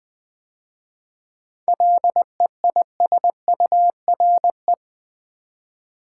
Morse code, heard as a keyed tone, about 20 wpm, 700 Hz. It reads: LEISURE
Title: 7 Letter Words - First 3 Letters Given - 20wpm